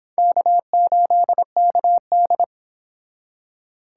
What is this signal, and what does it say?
Morse code, heard as a keyed tone, about 26 wpm, 700 Hz. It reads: K8KD